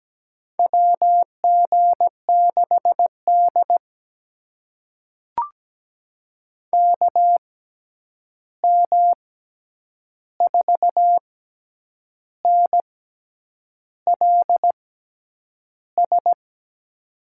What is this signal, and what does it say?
Morse code, heard as a keyed tone, about 17 wpm, spaced as for 8 wpm, 700 Hz. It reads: WG6D KM4NLS
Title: USA Call Signs - 6x Character Spacing - 17wpm